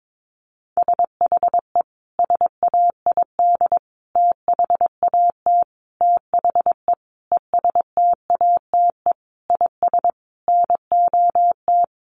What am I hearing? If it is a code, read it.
Morse code, heard as a keyed tone, about 22 wpm, 700 Hz. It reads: SHE SAID THAT THE ESTATE IS NOT
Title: Sentences from Top 1200 Words 22wpm